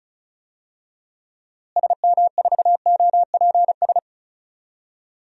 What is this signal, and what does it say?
Morse code, heard as a keyed tone, about 36 wpm, 700 Hz. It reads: SM4OPS
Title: DXCC-100 Call Signs - 6 Characters - Repeated 3x - 35wpm